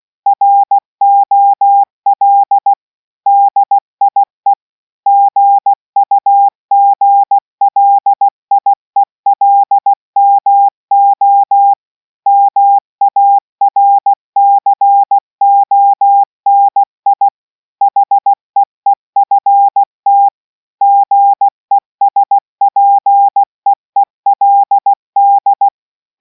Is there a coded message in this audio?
Yes: Morse code from 0.3 to 25.7 s, heard as a keyed tone, about 16 wpm, 800 Hz. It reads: ROL DIE GUGLIELMO MARCONI HEEFT GESPEELD